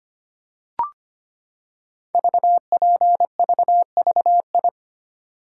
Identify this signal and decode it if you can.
Morse code, heard as a keyed tone, about 25 wpm, 700 Hz. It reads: VPVVI
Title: Code Groups - 5 Letters - 25wpm